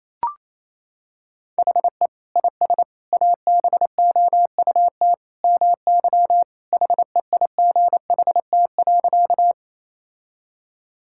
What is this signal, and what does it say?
Morse code, heard as a keyed tone, about 28 wpm, 700 Hz. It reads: HE IS ABOUT MY HEIGHT.
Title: Sentences from Top 1600 Words 28wpm